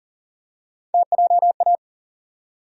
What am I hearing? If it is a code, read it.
Morse code, heard as a keyed tone, about 40 wpm, 700 Hz. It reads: TJA
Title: USA Call Sign Postfixes - 3 Letters 40wpm